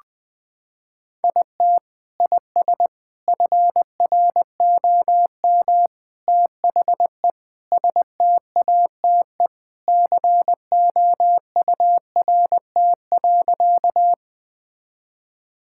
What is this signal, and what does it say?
Morse code, heard as a keyed tone, about 20 wpm, 700 Hz. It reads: IT IS FROM THE STATE COURT.
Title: Sentences from Top 1600 Words 20wpm